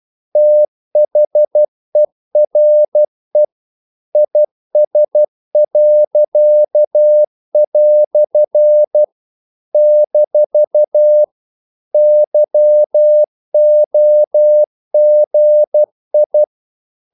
Morse code, heard as a keyed tone, about 12 wpm, 600 Hz. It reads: THERE IS." - YOGI
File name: CW / Practice / EnglishQuotations0112